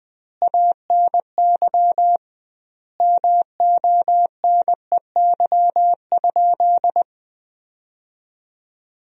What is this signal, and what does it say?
Morse code, heard as a keyed tone, about 20 wpm, 700 Hz. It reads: ANY MONEY?
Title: 2x Word-Spacing - Sentences from Top 400 Words - 20wpm